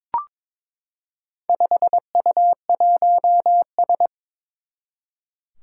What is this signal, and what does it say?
Morse code, heard as a keyed tone, about 22 wpm, 700 Hz. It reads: 5U1S
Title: Four Challenging Letter-Number Combinations 22wpm